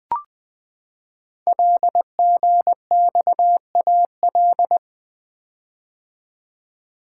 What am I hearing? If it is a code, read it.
Morse code, heard as a keyed tone, about 20 wpm, 700 Hz. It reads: LGXAL
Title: Code Groups - 5 Letters - 20wpm